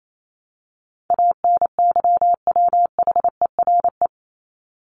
Morse code, heard as a keyed tone, about 28 wpm, 700 Hz. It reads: ANYWHERE
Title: Top 1251-1500 Words 28wpm